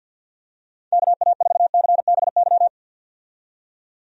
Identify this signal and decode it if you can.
Morse code, heard as a keyed tone, about 52 wpm, 700 Hz. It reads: KA4CBY
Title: USA Call Signs - 6 Characters - 50wpm